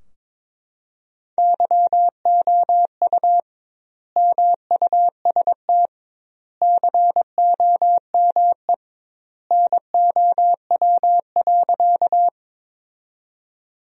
Morse code, heard as a keyed tone, about 21 wpm, 700 Hz. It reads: YOU MUST COME NOW.
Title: 2x Word-Spacing - Sentences from Top 200 Words - 22wpm